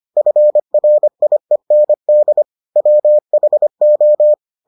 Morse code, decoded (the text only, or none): FRIEND WHO